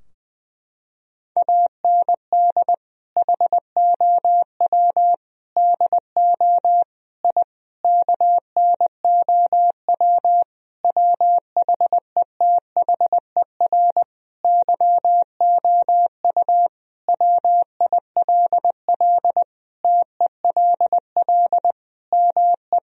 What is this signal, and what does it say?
Morse code, heard as a keyed tone, about 21 wpm, 700 Hz. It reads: AND HOW DO I KNOW WHETHER YOU WILL TELL ME
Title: Sentences from Top 1000 Words - Review Part 3 - 20wpm